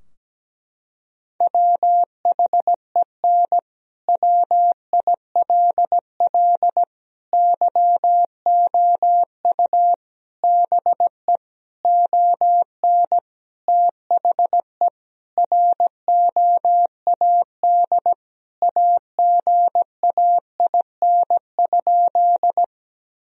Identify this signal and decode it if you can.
Morse code, heard as a keyed tone, about 17 wpm, 700 Hz. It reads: WHEN WILL YOU BE ON THE ROAD AGAIN?
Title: Sentences from Top 1000 Words - Review Part 2 - 17wpm